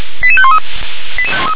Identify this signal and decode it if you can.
Five-tone selective call (ZVEI), heard as quick runs of five tones, tones of about 70 ms. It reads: 88513